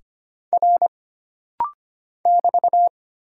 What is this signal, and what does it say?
Morse code, heard as a keyed tone, about 25 wpm, 700 Hz. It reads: R <BT>
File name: Single Character (Letter:Number:Punctuation:Prosign) - Rapid-Fire - 25wpm